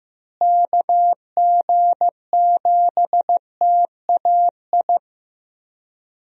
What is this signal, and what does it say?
Morse code, heard as a keyed tone, about 15 wpm, 700 Hz. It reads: KG7TAI
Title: USA Call Signs - 6 Characters - Repeated 3x - 15wpm